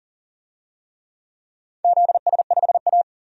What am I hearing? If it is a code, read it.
Morse code, heard as a keyed tone, about 40 wpm, 700 Hz. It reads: ZS5A
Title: DXCC-100 Call Signs - 4 Characters - 40wpm